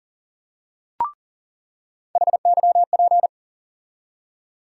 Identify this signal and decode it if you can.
Morse code, heard as a keyed tone, about 40 wpm, 700 Hz. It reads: HYP
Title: USA Call Sign Postfixes - 3 Letters 40wpm